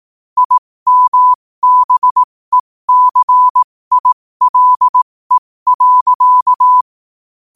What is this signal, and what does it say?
Morse code, heard as a keyed tone, about 18 wpm, spaced as for 15 wpm, 1000 Hz. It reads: IMBECILE.
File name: 15wpm